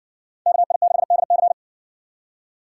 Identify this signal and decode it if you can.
Morse code, heard as a keyed tone, about 56 wpm, 700 Hz. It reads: KI6DC